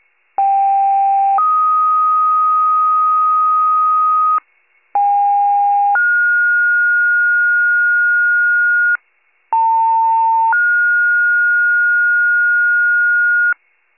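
Two-tone page, heard as a one-second tone, then a three-second tone, about 790 then 1250 Hz, about 810 then 1430 Hz, about 900 then 1430 Hz, with noise. It is heard from 0.4 to 13.5 s.
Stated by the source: County-wide two-tone paging alert.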